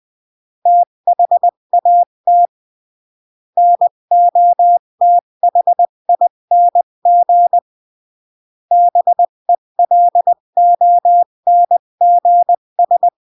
Morse code, heard as a keyed tone, about 20 wpm, 700 Hz. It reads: THAT NOTHING BELONGS
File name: CW / jodorowsky / 82 commandments 20-15